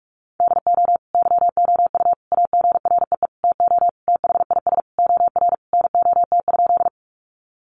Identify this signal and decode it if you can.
Morse code, heard as a keyed tone, about 44 wpm, 700 Hz. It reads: DO YOU AGREE TO THIS OR NOT?
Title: Sentences from Top 1100 Words 45wpm